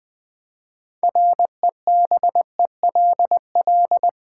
Morse code, heard as a keyed tone, about 20 wpm, 700 Hz. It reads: REBELL
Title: Build A Word - 3 Letters At A Time - 9-Letter Words - 20wpm